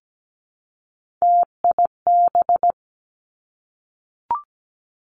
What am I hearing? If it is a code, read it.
Morse code, heard as a keyed tone, about 17 wpm, 700 Hz. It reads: TIB
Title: Three Challenging Letter-Number Combinations 17wpm